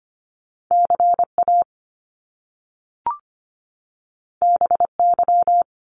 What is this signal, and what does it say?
Morse code, heard as a keyed tone, about 25 wpm, 700 Hz. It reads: CA BY
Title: Top Two Letter Combinations in Words 25wpm